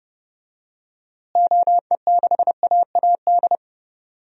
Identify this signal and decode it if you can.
Morse code, heard as a keyed tone, about 32 wpm, 700 Hz. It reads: OE6AAD
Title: DXCC-100 Call Signs - 6 Characters - Repeated 3x - 30wpm